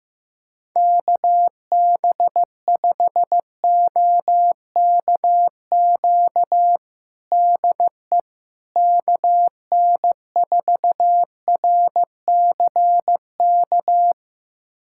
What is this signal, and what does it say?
Morse code, heard as a keyed tone, about 15 wpm, 700 Hz. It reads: KB5OKQ DE KN4RCK
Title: USA Callsign DE Callsign 15wpm